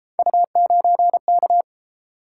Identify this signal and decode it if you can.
Morse code, heard as a keyed tone, about 30 wpm, 700 Hz. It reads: U9K